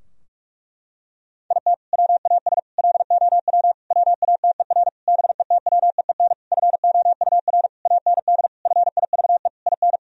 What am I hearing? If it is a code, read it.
Morse code, heard as a keyed tone, about 41 wpm, 700 Hz. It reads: IT WAS LOW WATER BETWEEN FOUR AND FIVE IN
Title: Sentences from Top 1000 Words - Review Part 2 - 45wpm